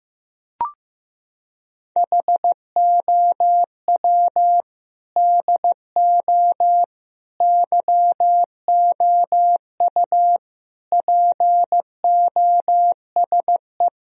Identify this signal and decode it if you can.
Morse code, heard as a keyed tone, about 15 wpm, 700 Hz. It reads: HOW DO YOU POSE
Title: Sentences from Top 1000 Words - Review Part 2 - 15wpm